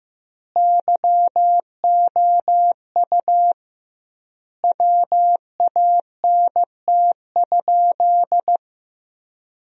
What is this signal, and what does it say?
Morse code, heard as a keyed tone, about 15 wpm, 700 Hz. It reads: YOU WANT?
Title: 2x Word-Spacing - Sentences from Top 200 Words - 15wpm